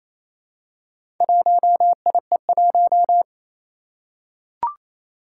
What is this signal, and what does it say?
Morse code, heard as a keyed tone, about 28 wpm, 700 Hz. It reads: 1IE1
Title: Four Challenging Letter-Number Combinations 28wpm